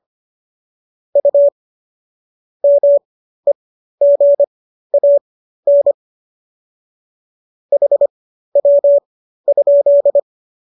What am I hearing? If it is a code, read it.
Morse code, heard as a keyed tone, about 25 wpm, spaced as for 13 wpm, 570 Hz. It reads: U MEGAN HW?